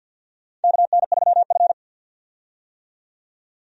Morse code, heard as a keyed tone, about 52 wpm, 700 Hz. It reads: KN3F
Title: USA Call Signs - 4 Characters - 50wpm